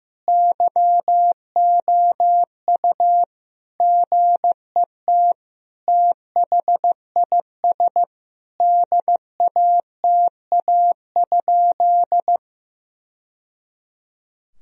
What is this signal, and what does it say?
Morse code, heard as a keyed tone, about 15 wpm, 700 Hz. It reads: YOU GET THIS DATA?
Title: Sentences from Top 1000 Words - Review Part 3 - 15wpm